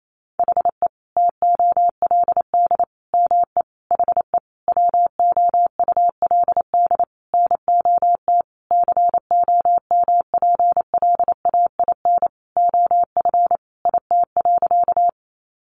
Morse code, heard as a keyed tone, about 28 wpm, 700 Hz. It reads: HE TOLD ME HE WOULD NOT COMPLAIN OF IT.
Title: Sentences from Top 1300 Words 28wpm